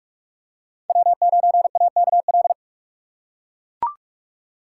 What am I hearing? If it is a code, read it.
Morse code, heard as a keyed tone, about 42 wpm, 700 Hz. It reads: W9AKL
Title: USA Call Signs - 5 Characters - 4x Character Spacing - 45wpm